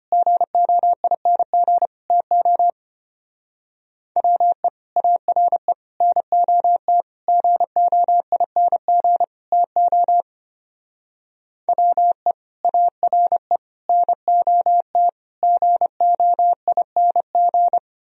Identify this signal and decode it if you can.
Morse code, heard as a keyed tone, about 32 wpm, 700 Hz. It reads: GOING TO WE ARE NOT GOING TO WE ARE NOT GOING